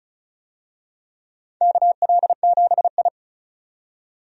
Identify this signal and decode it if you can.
Morse code, heard as a keyed tone, about 36 wpm, 700 Hz. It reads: KL7I
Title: DXCC-100 Call Signs - 4 Characters - 35wpm